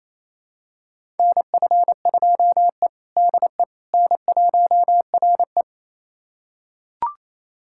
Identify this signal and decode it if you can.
Morse code, heard as a keyed tone, about 28 wpm, 700 Hz. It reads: NF2E DE N1RE
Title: USA Callsign DE Callsign - 4 Characters - 28wpm